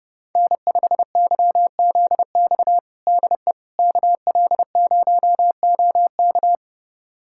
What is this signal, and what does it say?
Morse code, heard as a keyed tone, about 30 wpm, 700 Hz. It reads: N5YZX DE KL0OK
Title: USA Callsign DE Callsign - 5 Characters - 30wpm